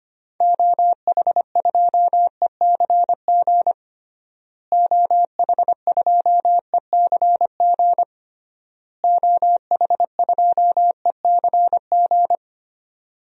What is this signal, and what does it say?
Morse code, heard as a keyed tone, about 25 wpm, 700 Hz. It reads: OH2ECG OH2ECG OH2ECG
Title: DXCC-100 Call Signs - 6 Characters - Repeated 3x - 25wpm